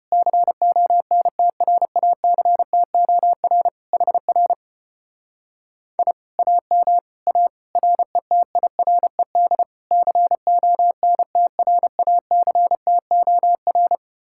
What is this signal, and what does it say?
Morse code, heard as a keyed tone, about 33 wpm, 700 Hz. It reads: CONTRACTOR HR I AM A RETIRED CONTRACTOR